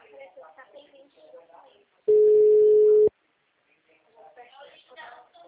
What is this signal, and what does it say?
Ringback tone (Europe, 425 Hz), with speech.
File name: out- -2006- - - .149